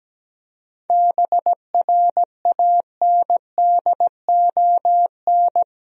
Morse code, heard as a keyed tone, about 17 wpm, 700 Hz. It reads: BRANDON
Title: USA Common Names - 17wpm